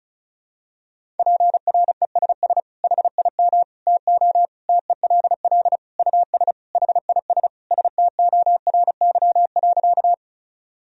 Morse code, heard as a keyed tone, about 34 wpm, 700 Hz. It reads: PRESS HIM TO TELL US HIS STORY.
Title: Sentences from Top 1000 Words - Review Part 1 - 35wpm